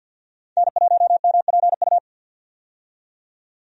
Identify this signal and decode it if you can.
Morse code, heard as a keyed tone, about 53 wpm, 700 Hz. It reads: N1MPU
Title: USA Call Signs - 5 Characters - 50wpm